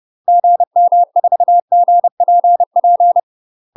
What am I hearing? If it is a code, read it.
Morse code, heard as a keyed tone, about 30 wpm, 700 Hz. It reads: GM4GPP
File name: UK MASTER QSOs TRAINING / 30WPM / UK MASTER QSOs 0033